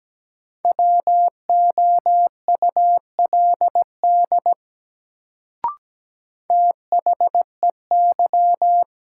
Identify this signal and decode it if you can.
Morse code, heard as a keyed tone, about 17 wpm, 700 Hz. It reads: WOULD THEY
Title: Sentences from Top 100 Words - New Word Drill - 17wpm